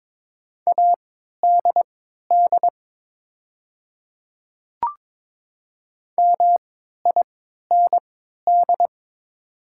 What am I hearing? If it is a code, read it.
Morse code, heard as a keyed tone, about 22 wpm, spaced as for 13 wpm, 700 Hz. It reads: ADD MIND
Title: 3x Character Spacing - Top 500 Words - 22wpm